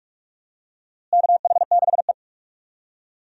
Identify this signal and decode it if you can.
Morse code, heard as a keyed tone, about 46 wpm, 700 Hz. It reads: KH6E
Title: DXCC-100 Call Signs - 4 Characters - Repeated 2x - 45wpm